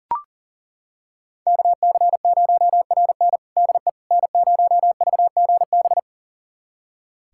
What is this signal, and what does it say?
Morse code, heard as a keyed tone, about 40 wpm, 700 Hz. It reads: KC0RN DE N0VGB
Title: USA Callsign DE Callsign 40wpm